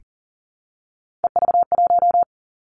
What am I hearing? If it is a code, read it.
Morse code, heard as a keyed tone, about 39 wpm, 690 Hz. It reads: EV1
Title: Three Challenging Letter-Number Combinations 40wpm